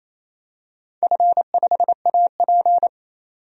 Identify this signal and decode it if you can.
Morse code, heard as a keyed tone, about 27 wpm, 690 Hz. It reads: F5AP
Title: DXCC-100 Call Signs - 4 Characters - 28wpm